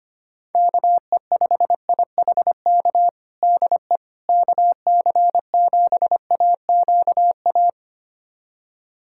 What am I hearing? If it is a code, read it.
Morse code, heard as a keyed tone, about 25 wpm, 700 Hz. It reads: KE5IHK DE KC7AQA
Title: USA Callsign DE Callsign - 6 Characters - 25wpm